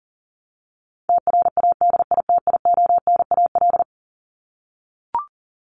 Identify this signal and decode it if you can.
Morse code, heard as a keyed tone, about 40 wpm, 700 Hz. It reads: TRADITIONAL
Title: Top 751-1000 Words 40wpm